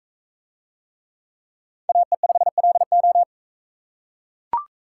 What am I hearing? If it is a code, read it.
Morse code, heard as a keyed tone, about 40 wpm, 700 Hz. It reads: AE5LO